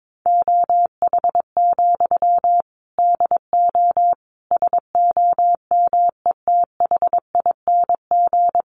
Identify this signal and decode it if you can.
Morse code, heard as a keyed tone, about 22 wpm, 700 Hz. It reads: OH, DO SOMETHING
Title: Sentences from Top 600 Words 22wpm